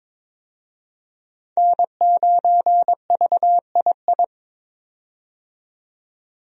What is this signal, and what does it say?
Morse code, heard as a keyed tone, about 21 wpm, 700 Hz. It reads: N9VII
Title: USA Call Signs - 5 Characters - 22wpm